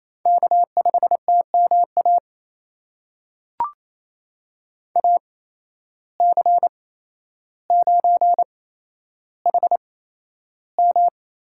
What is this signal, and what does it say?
Morse code, heard as a keyed tone, about 28 wpm, spaced as for 12 wpm, 700 Hz. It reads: K5TMA AC9HM
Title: USA Call Signs - 5 Characters - 8x Character Spacing - 28wpm